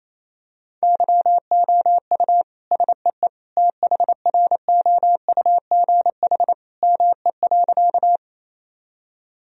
Morse code, heard as a keyed tone, about 28 wpm, 700 Hz. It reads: YOU SEE THROUGH ME.
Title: Sentences from Top 1000 Words - Review Part 1 - 28wpm